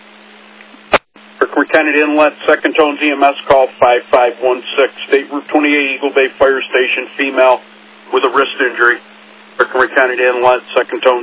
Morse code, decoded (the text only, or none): none